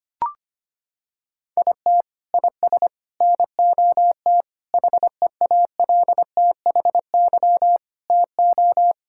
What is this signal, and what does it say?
Morse code, heard as a keyed tone, about 25 wpm, 700 Hz. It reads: IT IS NOT HEALTHY TO
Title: Sentences from Top 1000 Words - Review Part 6 - 25wpm